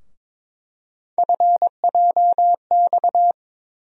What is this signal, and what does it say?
Morse code, heard as a keyed tone, about 23 wpm, 700 Hz. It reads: FJX
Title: USA Call Sign Postfixes - 3 Letters 22wpm